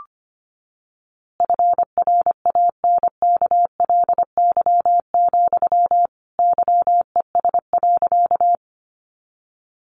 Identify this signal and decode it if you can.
Morse code, heard as a keyed tone, about 25 wpm, 700 Hz. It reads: FRANKLY, YES.
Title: A Sound Of Thunder 25wpm